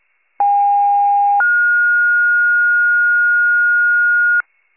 Two-tone page, heard as a one-second tone, then a three-second tone, about 810 then 1430 Hz, with noise. It is heard from 0.4 to 4.4 s.